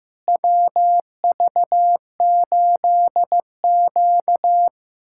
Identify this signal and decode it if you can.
Morse code, heard as a keyed tone, about 15 wpm, 700 Hz. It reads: WV8Q
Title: USA Callsign DE Callsign - 4 Characters - 15wpm